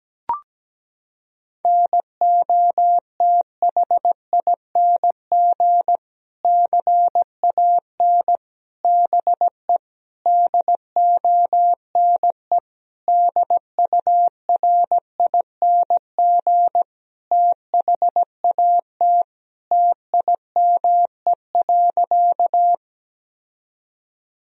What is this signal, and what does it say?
Morse code, heard as a keyed tone, about 17 wpm, 700 Hz. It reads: NOTHING CAN BE DONE DURING THAT TIME.
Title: Sentences from Top 1000 Words - Review Part 2 - 17wpm